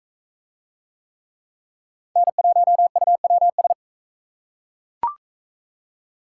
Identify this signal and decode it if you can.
Morse code, heard as a keyed tone, about 39 wpm, 700 Hz. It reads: N1UWS